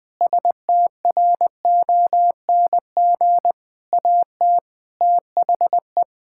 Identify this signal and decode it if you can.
Morse code, heard as a keyed tone, about 21 wpm, 700 Hz. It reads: STRONG AT THE
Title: Sentences with 6-Word Phrases - 20wpm